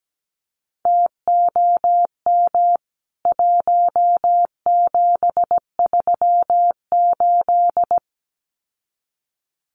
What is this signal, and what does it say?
Morse code, heard as a keyed tone, about 17 wpm, 700 Hz. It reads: TOM 1738
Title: CWT Exchange 17wpm